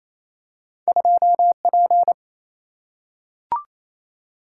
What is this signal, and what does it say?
Morse code, heard as a keyed tone, about 28 wpm, 700 Hz. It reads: 2P
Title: Single-Letter Single-Number 28wpm